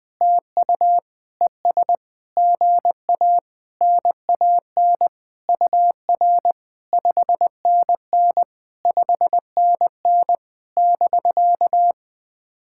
Morse code, heard as a keyed tone, about 20 wpm, 700 Hz. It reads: TU ES GA NAN UR 5NN 5NN <BK>
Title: POTA QSO Practice - 20wpm